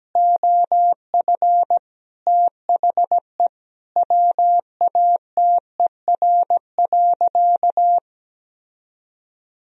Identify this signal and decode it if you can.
Morse code, heard as a keyed tone, about 17 wpm, 700 Hz. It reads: OF THE WATER.
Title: Sentences from Top 1000 Words - Review Part 3 - 17wpm